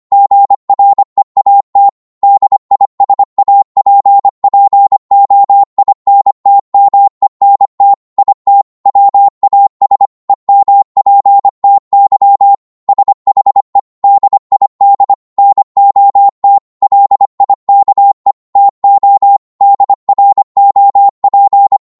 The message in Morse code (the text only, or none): GREAT DISAPPOINTMENT IT WAS EMPTY SHE DID NOT LIKE TO DROP